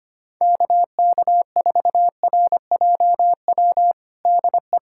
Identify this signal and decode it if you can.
Morse code, heard as a keyed tone, about 25 wpm, 700 Hz. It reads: KK4RJW DE
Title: USA Callsign DE Callsign 25wpm